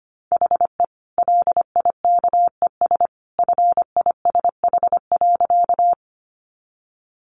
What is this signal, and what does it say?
Morse code, heard as a keyed tone, about 25 wpm, 700 Hz. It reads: HE LIKES FISH.
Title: Additional Easy Phrases 25wpm